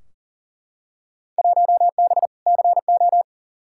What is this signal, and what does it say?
Morse code, heard as a keyed tone, about 39 wpm, 700 Hz. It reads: 1B CO
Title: Field Day Exchanges - 40wpm